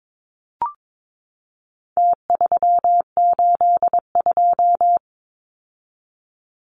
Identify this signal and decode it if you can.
Morse code, heard as a keyed tone, about 22 wpm, 700 Hz. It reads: T382
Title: Numbers - Contest Style - 4-Digit 22wpm